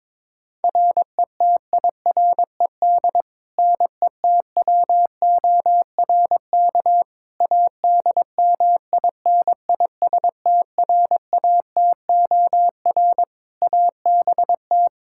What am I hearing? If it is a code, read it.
Morse code, heard as a keyed tone, about 22 wpm, 700 Hz. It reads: RETIRED NETWORK ADMINISTRATOR ABT
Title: K1USN SST QSO Practice 22wpm